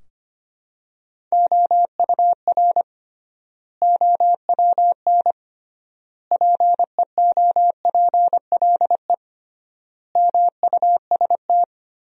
Morse code, heard as a keyed tone, about 25 wpm, 700 Hz. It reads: OUR OWN PEOPLE MUST
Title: 3x Word-Spacing - Sentences from Top 300 Words - 25wpm